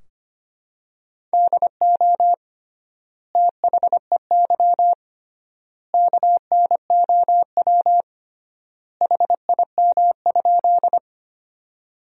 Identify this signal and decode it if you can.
Morse code, heard as a keyed tone, about 25 wpm, 700 Hz. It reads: DO THEY KNOW HIM?
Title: 3x Word-Spacing - Sentences from Top 100 Words - 25wpm